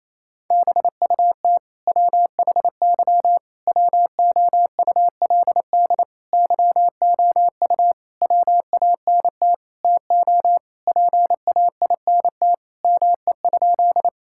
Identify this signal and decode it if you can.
Morse code, heard as a keyed tone, about 28 wpm, 700 Hz. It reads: BUT WHY WOULD YOU WANT TO PAINT ME?
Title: Sentences from Top 1000 Words - Review Part 6 - 28wpm